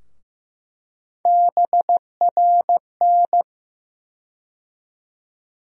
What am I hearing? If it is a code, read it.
Morse code, heard as a keyed tone, about 15 wpm, 700 Hz. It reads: BRN
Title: USA Call Sign Postfixes - 3 Letters 15wpm